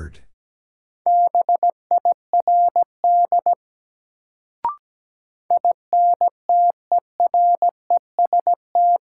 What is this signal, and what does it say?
Morse code, heard as a keyed tone, about 17 wpm, 700 Hz. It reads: BIRD INTEREST
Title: Sentences from Top 400 Words - New Word Drill - 17wpm